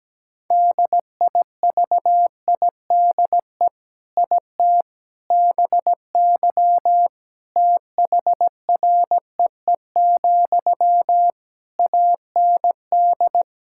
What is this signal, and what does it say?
Morse code, heard as a keyed tone, about 17 wpm, 700 Hz. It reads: DIVIDE IT BY THREE, AND
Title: Sentences from Top 1600 Words 17wpm